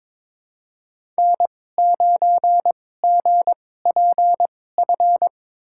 Morse code, heard as a keyed tone, about 22 wpm, spaced as for 16 wpm, 700 Hz. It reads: N9GPF
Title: USA Call Signs - 5 Characters - 2x Character Spacing - 22wpm